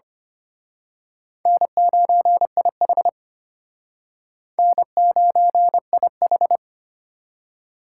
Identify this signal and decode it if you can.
Morse code, heard as a keyed tone, about 31 wpm, 700 Hz. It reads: N9IH N9IH
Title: Speed Racing - USA 1x2 Call Signs 25wpm